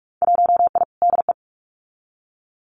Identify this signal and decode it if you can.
Morse code, heard as a keyed tone, about 46 wpm, 690 Hz. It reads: 1I DE
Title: Winter Field Day Exchanges Encore 45wpm